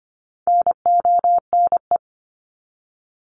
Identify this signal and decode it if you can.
Morse code, heard as a keyed tone, about 25 wpm, 700 Hz. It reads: NONE
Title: Top 1501-1750 Words 25wpm